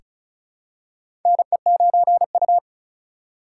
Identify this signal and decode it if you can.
Morse code, heard as a keyed tone, about 36 wpm, 700 Hz. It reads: NE9U
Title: K1USN SST QSO Practice - 35wpm